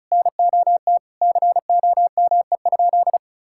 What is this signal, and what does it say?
Morse code, heard as a keyed tone, about 36 wpm, 700 Hz. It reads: NOT COME?
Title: Sentences from Top 1600 Words 35wpm